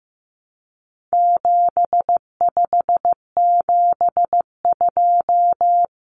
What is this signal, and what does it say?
Morse code, heard as a keyed tone, about 15 wpm, 700 Hz. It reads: 7572
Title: Numbers - 4-Digit 15wpm